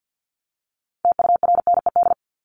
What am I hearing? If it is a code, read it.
Morse code, heard as a keyed tone, about 50 wpm, 700 Hz. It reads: TURNED
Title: ED Suffix - Top 100 Words with Base Words 50wpm